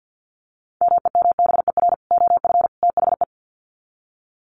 Morse code, heard as a keyed tone, about 50 wpm, 700 Hz. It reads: MEMBER OF THE
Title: Sets of 3 Words 50wpm